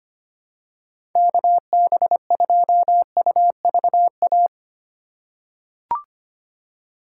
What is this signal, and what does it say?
Morse code, heard as a keyed tone, about 25 wpm, 700 Hz. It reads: KB2UVA
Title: USA Call Signs - 6 Characters - 2x Character Spacing - 25wpm